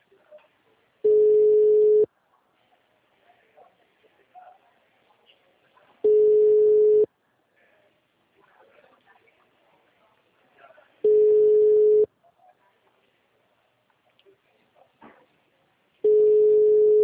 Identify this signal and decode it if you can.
Ringback tone (Europe, 425 Hz).
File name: out- -2001- - - .3051